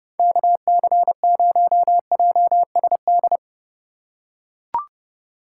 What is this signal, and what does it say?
Morse code, heard as a keyed tone, about 30 wpm, 700 Hz. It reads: KC0JSD